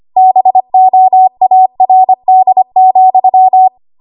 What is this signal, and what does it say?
Morse code, heard as a keyed tone, about 25 wpm, 750 Hz. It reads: BOARD,
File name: Morse / 25WPM